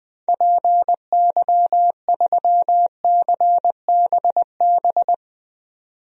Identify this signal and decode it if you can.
Morse code, heard as a keyed tone, about 20 wpm, 700 Hz. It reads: PY3CBB
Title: DXCC-100 Call Signs - 6 Characters - 20wpm